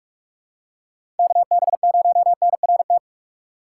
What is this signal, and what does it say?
Morse code, heard as a keyed tone, about 44 wpm, 700 Hz. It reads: KB0NRT
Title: USA Call Signs - 6 Characters - 6x Character Spacing - 45wpm